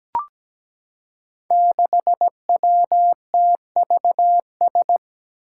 Morse code, heard as a keyed tone, about 17 wpm, 700 Hz. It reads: 6WTVS
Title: Five Challenging Letter-Number Combinations 17wpm